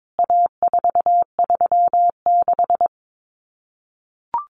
Morse code, heard as a keyed tone, about 22 wpm, 700 Hz. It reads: A436
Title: CWT Exchange 22wpm